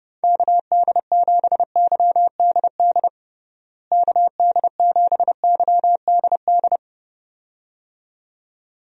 Morse code, heard as a keyed tone, about 30 wpm, 700 Hz. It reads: KD7YDD KD7YDD
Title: USA Call Signs - 6 Characters - Repeated 2x - 30wpm